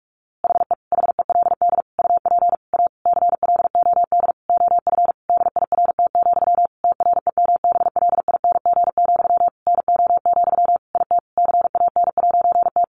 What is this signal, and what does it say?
Morse code, heard as a keyed tone, about 44 wpm, 700 Hz. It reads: HE HELD UP A CLOD OF DIRT, TREMBLING, NO, IT CAN'T
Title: A Sound Of Thunder 45wpm